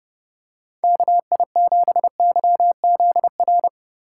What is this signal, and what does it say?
Morse code, heard as a keyed tone, about 30 wpm, 700 Hz. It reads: KI7YZR